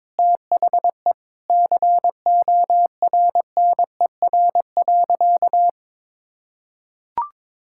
Morse code, heard as a keyed tone, about 22 wpm, 700 Hz. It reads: THE CORNER.
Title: Sentences from Top 1600 Words 22wpm